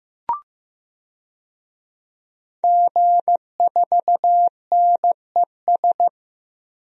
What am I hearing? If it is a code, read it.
Morse code, heard as a keyed tone, about 15 wpm, 700 Hz. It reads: G4NES
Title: DXCC-100 Call Signs - 5 Characters - Repeated 2x - 15wpm